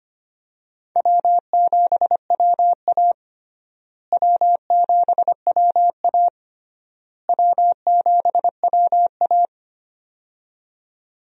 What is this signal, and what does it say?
Morse code, heard as a keyed tone, about 25 wpm, 700 Hz. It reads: W7WA W7WA W7WA
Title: USA Call Signs - 4 Characters - Repeated 3x - 25wpm